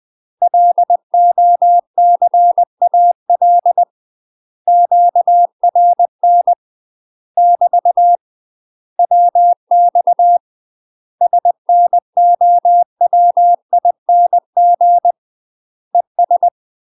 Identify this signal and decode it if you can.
Morse code, heard as a keyed tone, about 20 wpm, 700 Hz. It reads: LOCAL QRN <BT> WX SNOWING ES
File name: UK MASTER QSOs TRAINING / 20WPM / UK MASTER QSOs 0039